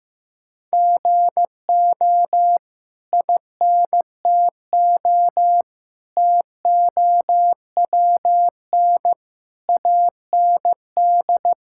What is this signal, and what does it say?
Morse code, heard as a keyed tone, about 15 wpm, 700 Hz. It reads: GO INTO TOWN AND
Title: Sentences from Top 1000 Words - Review Part 6 - 15wpm